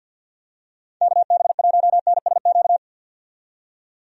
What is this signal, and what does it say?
Morse code, heard as a keyed tone, about 52 wpm, 700 Hz. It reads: KB1NSQ
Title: USA Call Signs - 6 Characters - 50wpm